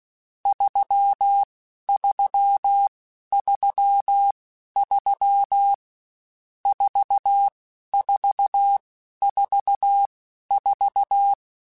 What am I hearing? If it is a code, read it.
Morse code, heard as a keyed tone, about 16 wpm, spaced as for 11 wpm, 800 Hz. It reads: 3333 4444
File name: Misc / Misc-mp3s / Numb @15